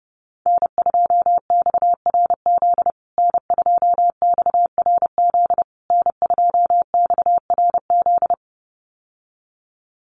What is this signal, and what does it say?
Morse code, heard as a keyed tone, about 30 wpm, 700 Hz. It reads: N2XRZ N2XRZ N2XRZ
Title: USA Call Signs Repeated 3x 30wpm